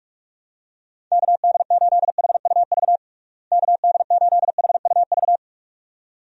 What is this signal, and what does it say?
Morse code, heard as a keyed tone, about 41 wpm, 700 Hz. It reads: KD8HUV KD8HUV
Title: USA Call Signs - 6 Characters - Repeated 2x - 45wpm